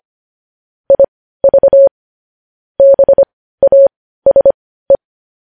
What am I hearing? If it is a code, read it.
Morse code, heard as a keyed tone, about 25 wpm, spaced as for 15 wpm, 550 Hz. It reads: IV BASE
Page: news headlines in morse code at 15 wpm